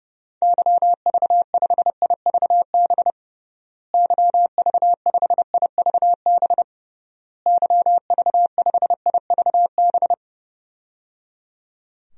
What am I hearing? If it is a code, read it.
Morse code, heard as a keyed tone, about 32 wpm, 700 Hz. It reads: YV5IVB YV5IVB YV5IVB
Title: DXCC-100 Call Signs - 6 Characters - Repeated 3x - 30wpm